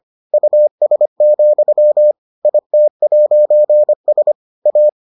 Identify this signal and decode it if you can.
Morse code, heard as a keyed tone, about 25 wpm, 600 Hz. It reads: US, IT'S A